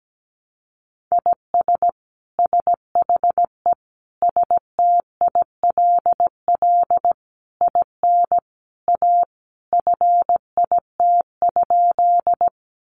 Morse code, heard as a keyed tone, about 17 wpm, 700 Hz. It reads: IS SHE STILL IN A FIT?
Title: Sentences from Top 800 Words 17wpm